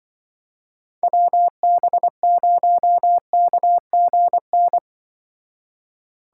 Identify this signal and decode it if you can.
Morse code, heard as a keyed tone, about 24 wpm, 700 Hz. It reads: WB0KGN
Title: Speed Racing - USA Call Signs 20wpm